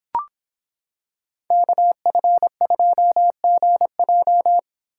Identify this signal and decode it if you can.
Morse code, heard as a keyed tone, about 25 wpm, 700 Hz. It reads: KF2GJ